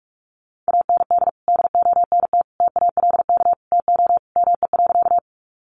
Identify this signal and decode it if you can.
Morse code, heard as a keyed tone, about 44 wpm, 700 Hz. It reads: AND DONT TALK TO ME.
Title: Sentences from Top 400 Words 45wpm